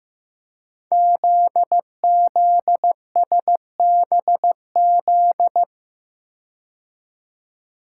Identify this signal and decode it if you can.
Morse code, heard as a keyed tone, about 15 wpm, 700 Hz. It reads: ZZSBZ
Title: Code Groups - 5 Letters - 15wpm